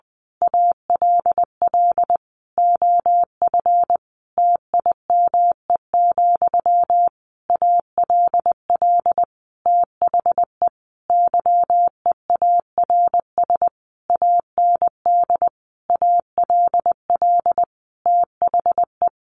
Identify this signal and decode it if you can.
Morse code, heard as a keyed tone, about 20 wpm, 700 Hz. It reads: ALL OF TIME, ALL THE YEARS AND ALL THE
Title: A Sound Of Thunder 20wpm